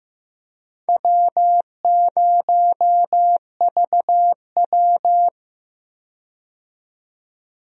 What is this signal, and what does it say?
Morse code, heard as a keyed tone, about 15 wpm, 700 Hz. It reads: W0VW
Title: USA Call Signs - 4 Characters - 15wpm